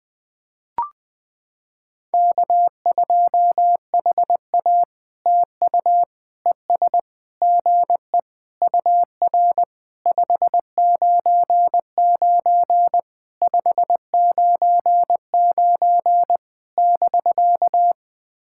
Morse code, heard as a keyed tone, about 20 wpm, 700 Hz. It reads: K2HA TU ES GE UR 599 599 <BK>
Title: POTA QSO Practice - 20wpm